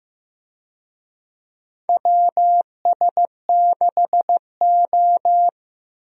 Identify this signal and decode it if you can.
Morse code, heard as a keyed tone, about 16 wpm, 700 Hz. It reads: WS6O
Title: USA Call Signs - 4 Characters - Repeated 3x - 15wpm